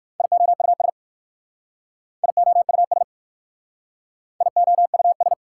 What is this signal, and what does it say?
Morse code, heard as a keyed tone, about 53 wpm, 690 Hz. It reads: IOUS IOUS IOUS
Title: Speed-Racing Prefixes and Suffixes 45wpm